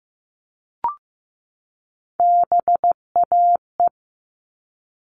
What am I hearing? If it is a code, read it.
Morse code, heard as a keyed tone, about 15 wpm, 700 Hz. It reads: BAE
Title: All Cut Numbers - 3-Digit 15wpm